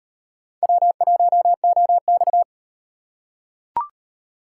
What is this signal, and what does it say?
Morse code, heard as a keyed tone, about 36 wpm, 700 Hz. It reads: W1OX